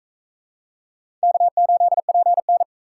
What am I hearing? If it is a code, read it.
Morse code, heard as a keyed tone, about 41 wpm, 700 Hz. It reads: K8PN